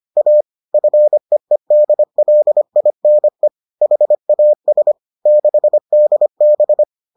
Morse code, heard as a keyed tone, about 25 wpm, 600 Hz. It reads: A FEEDLINE HAS 6DB